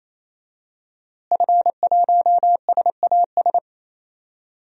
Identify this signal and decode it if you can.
Morse code, heard as a keyed tone, about 28 wpm, 700 Hz. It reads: F1SAS
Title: DXCC-100 Call Signs - 5 Characters - Repeated 3x - 28wpm